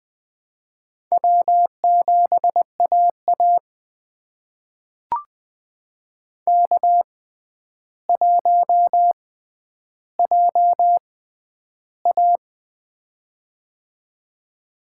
Morse code, heard as a keyed tone, about 20 wpm, spaced as for 10 wpm, 700 Hz. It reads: W7AA K1JA
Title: USA Call Signs - 4 Characters - 6x Character Spacing - 20wpm